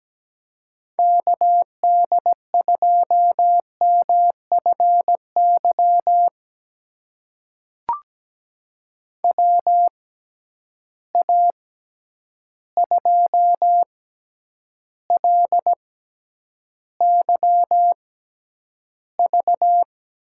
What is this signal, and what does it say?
Morse code, heard as a keyed tone, about 17 wpm, spaced as for 9 wpm, 700 Hz. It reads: KD2MFY WA2LYV
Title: USA Call Signs - 6 Characters - 6x Character Spacing - 17wpm